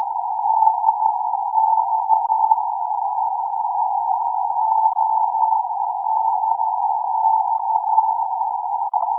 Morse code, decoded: none